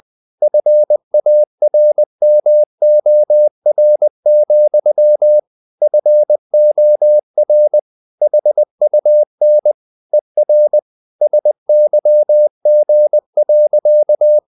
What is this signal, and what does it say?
Morse code, heard as a keyed tone, about 20 wpm, 600 Hz. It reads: FARMOR, FOR HUN ER SYG.
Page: Ferie 20 wpm